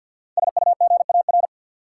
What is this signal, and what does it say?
Morse code, heard as a keyed tone, about 49 wpm, 700 Hz. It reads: SUGAR
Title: Sentences from Top 1200 Words - New Word Drill - 50wpm